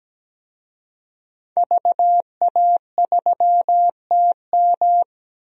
Morse code, heard as a keyed tone, about 17 wpm, 700 Hz. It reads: VA3TM
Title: DXCC-100 Call Signs - 5 Characters - Repeated 2x - 17wpm